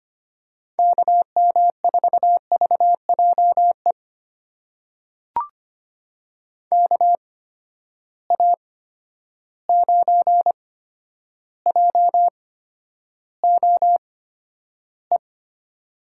Morse code, heard as a keyed tone, about 25 wpm, spaced as for 11 wpm, 700 Hz. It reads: KM4VJE KA9JOE
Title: USA Call Signs - 6 Characters - 8x Character Spacing - 25wpm